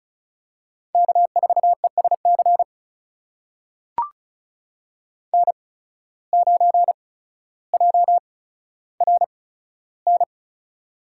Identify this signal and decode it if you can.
Morse code, heard as a keyed tone, about 36 wpm, spaced as for 15 wpm, 700 Hz. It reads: K4ESC N9JRN
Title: USA Call Signs - 5 Characters - 8x Character Spacing - 35wpm